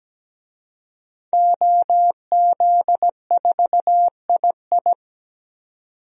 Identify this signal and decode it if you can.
Morse code, heard as a keyed tone, about 17 wpm, 700 Hz. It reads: OZ4II
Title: DXCC-100 Call Signs - 5 Characters - 17wpm